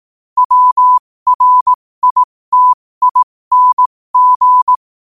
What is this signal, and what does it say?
Morse code, heard as a keyed tone, about 18 wpm, spaced as for 15 wpm, 1000 Hz. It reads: WRITING
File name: qotdfiles / 15wpm